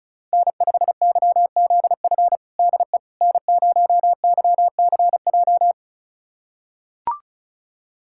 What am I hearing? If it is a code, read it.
Morse code, heard as a keyed tone, about 36 wpm, 700 Hz. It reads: N5YZF DE N0YCJ
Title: USA Callsign DE Callsign - 5 Characters - 35wpm